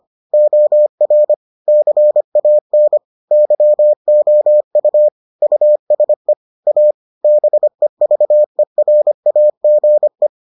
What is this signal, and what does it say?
Morse code, heard as a keyed tone, about 25 wpm, 600 Hz. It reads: OR CAN YOU USE A BEVERAGE